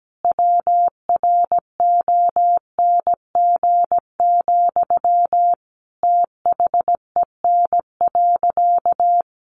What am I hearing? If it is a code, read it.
Morse code, heard as a keyed tone, about 17 wpm, 700 Hz. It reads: WRONG, THEN.
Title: Sentences from Top 900 Words 17wpm